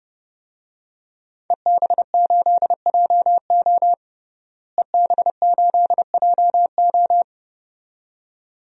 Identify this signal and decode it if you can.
Morse code, heard as a keyed tone, about 30 wpm, 700 Hz. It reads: EB8JO EB8JO
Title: DXCC-100 Call Signs - 5 Characters - Repeated 2x - 30wpm